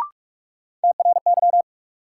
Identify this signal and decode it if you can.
Morse code, heard as a keyed tone, about 42 wpm, 700 Hz. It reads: TRY
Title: Sentences from Top 200 Words - New Word Drill - 45wpm